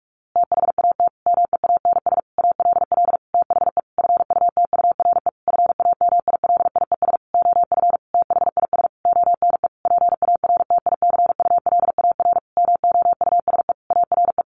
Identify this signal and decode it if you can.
Morse code, heard as a keyed tone, about 44 wpm, 700 Hz. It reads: THAT MEANS ALL THE FUTURE FAMILIES OF THIS ONE PARTICULAR MOUSE ARE
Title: A Sound Of Thunder 45wpm